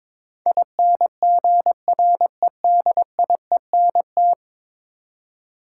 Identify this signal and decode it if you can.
Morse code, heard as a keyed tone, about 22 wpm, 700 Hz. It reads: INGREDIENT
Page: morse-code-ninja-practice-22wpm